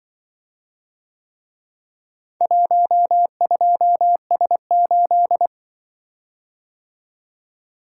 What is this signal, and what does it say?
Morse code, heard as a keyed tone, about 24 wpm, 700 Hz. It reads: 12S8
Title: Speed-Racing - Four Letter-Number Combinations - 20wpm